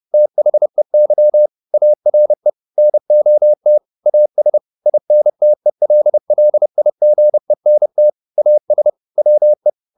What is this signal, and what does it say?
Morse code, heard as a keyed tone, about 30 wpm, 600 Hz. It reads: THEY ARE NOT AS INTELLIGENT AS WE
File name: the old man and the sea / the old man and the sea 1 30wpm